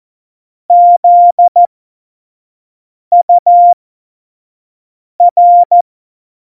Morse code, heard as a keyed tone, about 14 wpm, spaced as for 5 wpm, 700 Hz. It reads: ZUR
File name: Morsekurs / Witze / Witz 21. November 2017, 25 F70